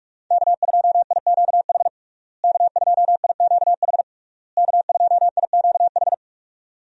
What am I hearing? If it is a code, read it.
Morse code, heard as a keyed tone, about 44 wpm, 700 Hz. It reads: K2IQH K2IQH K2IQH
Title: USA Call Signs - 5 Characters - Repeated 3x - 45wpm